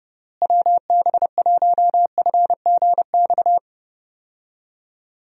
Morse code, heard as a keyed tone, about 32 wpm, 700 Hz. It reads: WB1FGX
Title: USA Call Signs - 6 Characters - Repeated 3x - 30wpm